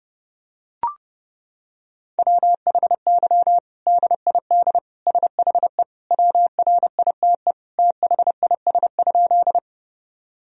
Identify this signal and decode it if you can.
Morse code, heard as a keyed tone, about 30 wpm, 700 Hz. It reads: WHY DID SHE WRITE THIS?
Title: Sentences from Top 1000 Words - Review Part 1 - 30wpm